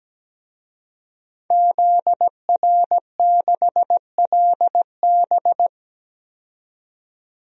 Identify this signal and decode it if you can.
Morse code, heard as a keyed tone, about 17 wpm, 700 Hz. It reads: ZR6LB
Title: DXCC-100 Call Signs - 5 Characters - 17wpm